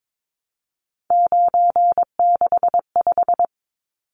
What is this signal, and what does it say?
Morse code, heard as a keyed tone, about 22 wpm, 700 Hz. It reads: 965
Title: Numbers - 3-Digit 22wpm